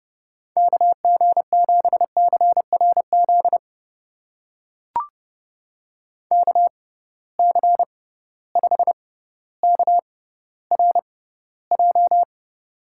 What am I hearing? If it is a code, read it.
Morse code, heard as a keyed tone, about 31 wpm, spaced as for 16 wpm, 700 Hz. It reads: KG7CRZ KC5KRJ
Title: USA Call Signs - 6 Characters - 6x Character Spacing - 30wpm